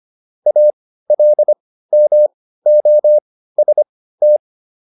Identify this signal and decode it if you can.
Morse code, heard as a keyed tone, about 25 wpm, spaced as for 15 wpm, 600 Hz. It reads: ALMOST